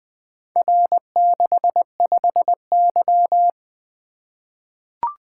Morse code, heard as a keyed tone, about 20 wpm, 700 Hz. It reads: R65Y
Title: Four Letter-Number Combinations - 20wpm